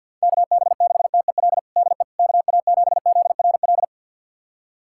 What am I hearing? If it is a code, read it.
Morse code, heard as a keyed tone, about 50 wpm, 690 Hz. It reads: KB6TEL DE KA7ZRL
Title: USA Callsign DE Callsign 50wpm